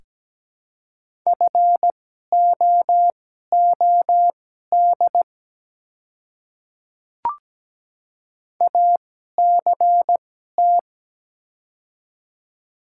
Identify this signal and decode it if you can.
Morse code, heard as a keyed tone, about 17 wpm, spaced as for 12 wpm, 690 Hz. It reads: FOOD ACT
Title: 2x Character Spacing - Top 300 Words - 17wpm